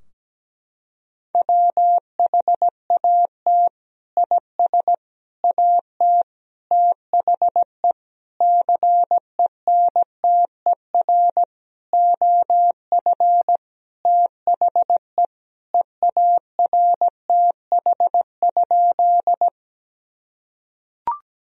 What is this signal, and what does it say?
Morse code, heard as a keyed tone, about 17 wpm, 700 Hz. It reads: WHAT IS AT THE CENTER OF THE EARTH?
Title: Sentences from Top 1000 Words - Review Part 2 - 17wpm